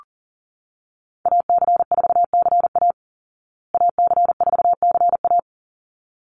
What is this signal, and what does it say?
Morse code, heard as a keyed tone, about 40 wpm, 700 Hz. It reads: AC4CA AC4CA
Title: Call Signs Repeated 3x - 40wpm